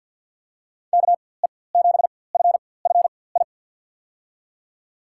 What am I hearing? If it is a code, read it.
Morse code, heard as a keyed tone, about 48 wpm, spaced as for 23 wpm, 700 Hz. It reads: KE7FFI
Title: USA Call Signs - 6 Characters - 4x Character Spacing - 50wpm